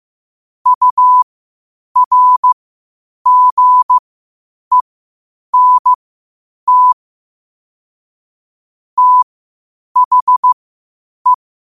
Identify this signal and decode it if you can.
Morse code, heard as a keyed tone, about 15 wpm, spaced as for 8 wpm, 1000 Hz. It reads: URGENT THE
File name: qotdfiles / 8wpm